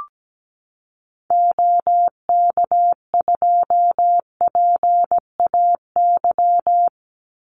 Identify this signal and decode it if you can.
Morse code, heard as a keyed tone, about 17 wpm, 700 Hz. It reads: OK2PAY
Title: Call Signs - 17wpm